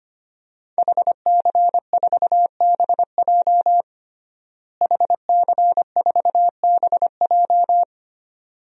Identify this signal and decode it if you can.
Morse code, heard as a keyed tone, about 25 wpm, 700 Hz. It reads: HC4BJ HC4BJ
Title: DXCC-100 Call Signs - 5 Characters - Repeated 2x - 25wpm